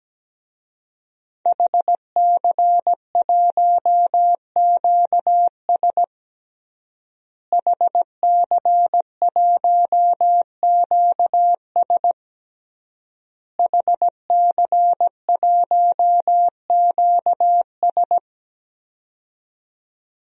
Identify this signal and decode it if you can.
Morse code, heard as a keyed tone, about 17 wpm, 700 Hz. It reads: HC1QS HC1QS HC1QS
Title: DXCC-100 Call Signs - 5 Characters - Repeated 3x - 17wpm